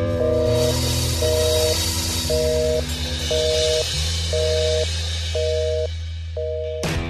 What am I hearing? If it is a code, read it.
Busy tone (North America), 480 + 620 Hz.